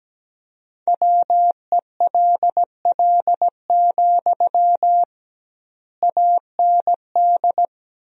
Morse code, heard as a keyed tone, about 17 wpm, 700 Hz. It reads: WELL, AND
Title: 2x Word-Spacing - Sentences from Top 200 Words - 17wpm